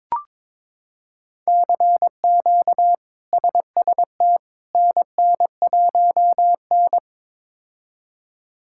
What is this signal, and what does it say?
Morse code, heard as a keyed tone, about 22 wpm, 700 Hz. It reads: CQ SST NN1N
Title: K1USN SST QSO Practice - 22wpm